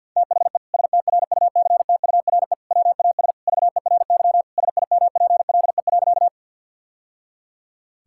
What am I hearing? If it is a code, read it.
Morse code, heard as a keyed tone, about 50 wpm, 700 Hz. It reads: THE STRUCTURE WAS VERY SIMPLE.
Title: Sentences from Top 1000 Words - Review Part 6 - 50wpm